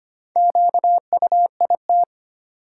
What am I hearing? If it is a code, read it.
Morse code, heard as a keyed tone, about 25 wpm, 700 Hz. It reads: QUIT
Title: Four Letter Words - 25wpm